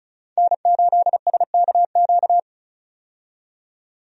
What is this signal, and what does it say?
Morse code, heard as a keyed tone, about 36 wpm, 700 Hz. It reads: N8SKQ
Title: USA Call Signs - 5 Characters - 35wpm